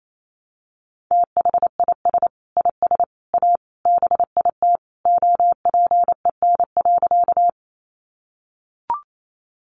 Morse code, heard as a keyed tone, about 28 wpm, 700 Hz. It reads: THIS IS A BIT OPEN.
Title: Sentences from Top 900 Words 28wpm